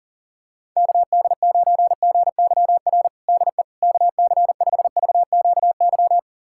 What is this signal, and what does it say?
Morse code, heard as a keyed tone, about 40 wpm, 700 Hz. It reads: KD9GYR DE KC5VQY
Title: USA Callsign DE Callsign - 6 Characters - 40wpm